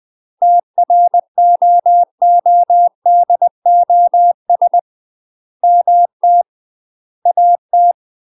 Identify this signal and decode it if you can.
Morse code, heard as a keyed tone, about 20 wpm, 700 Hz. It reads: TROODOS MT AT